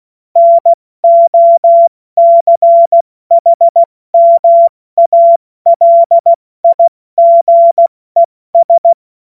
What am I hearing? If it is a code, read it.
Morse code, heard as a keyed tone, about 16 wpm, 680 Hz. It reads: NOCHMALIGES
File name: Witz-02-September-2025-70-F80